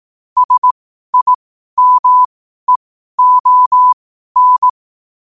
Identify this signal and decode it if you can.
Morse code, heard as a keyed tone, about 18 wpm, spaced as for 12 wpm, 1000 Hz. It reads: SIMEON
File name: qotdfiles / 12wpm